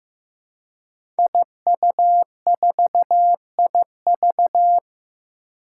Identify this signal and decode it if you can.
Morse code, heard as a keyed tone, about 15 wpm, 700 Hz. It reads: IU4IV
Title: Five Challenging Letter-Number Combinations 15wpm